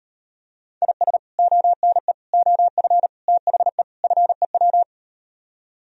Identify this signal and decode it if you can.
Morse code, heard as a keyed tone, about 39 wpm, 700 Hz. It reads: IS ONE OF THE FEW